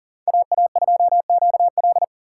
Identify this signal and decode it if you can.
Morse code, heard as a keyed tone, about 40 wpm, 700 Hz. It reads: AA2QL
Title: USA Call Signs - 5 Characters - 3x Character Spacing - 40wpm